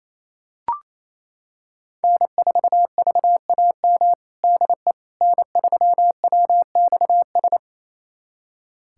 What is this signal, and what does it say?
Morse code, heard as a keyed tone, about 28 wpm, 700 Hz. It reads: N4VAM DE N3WXS
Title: USA Callsign DE Callsign - 5 Characters - 28wpm